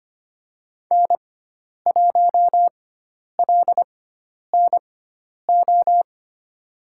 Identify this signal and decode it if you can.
Morse code, heard as a keyed tone, about 25 wpm, spaced as for 10 wpm, 700 Hz. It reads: N1LNO
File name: Lesson 16g - Letter c - Call Signs 25wpm